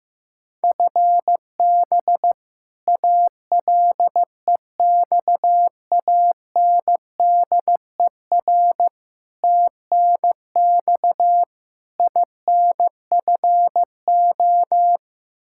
Morse code, heard as a keyed tone, about 15 wpm, 700 Hz. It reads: FB ALEXANDER TNX INFO
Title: K1USN SST QSO Practice 15wpm